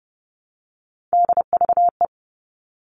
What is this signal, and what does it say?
Morse code, heard as a keyed tone, about 30 wpm, 700 Hz. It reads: DVE
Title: All Cut Numbers - 3-Digit 30wpm